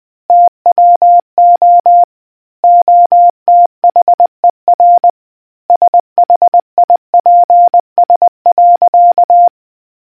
Morse code, heard as a keyed tone, about 20 wpm, 700 Hz. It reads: TWO OTHER SHIPS.